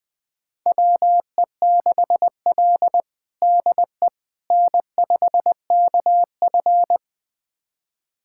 Morse code, heard as a keyed tone, about 20 wpm, 700 Hz. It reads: WE6L DE N5KF
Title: USA Callsign DE Callsign - 4 Characters - 20wpm